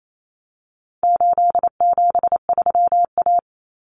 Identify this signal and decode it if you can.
Morse code, heard as a keyed tone, about 28 wpm, 700 Hz. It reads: 873A
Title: Common Cut Numbers - 4-Digit 28wpm